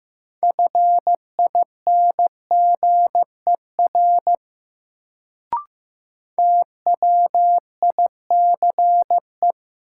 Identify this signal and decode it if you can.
Morse code, heard as a keyed tone, about 15 wpm, 700 Hz. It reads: FINGER TWICE
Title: Sentences from Top 1600 Words - New Word Drill Alt - 15wpm